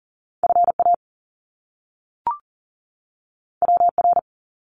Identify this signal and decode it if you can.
Morse code, heard as a keyed tone, about 40 wpm, 700 Hz. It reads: FA WR
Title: Top Two Letter Combinations in Words 40wpm